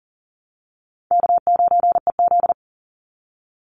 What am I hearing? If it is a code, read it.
Morse code, heard as a keyed tone, about 40 wpm, 700 Hz. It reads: K9EZ
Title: NA SOTA Chaser Callsigns 40wpm